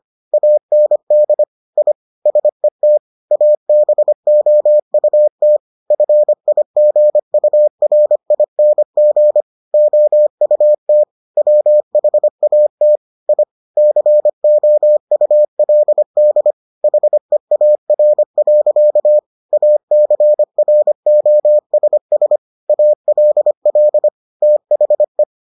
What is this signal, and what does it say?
Morse code, heard as a keyed tone, about 25 wpm, 600 Hz. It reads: AND I SET ABOUT FIGURING OUT WHAT I COULD HEAR. ACROSS ALL THE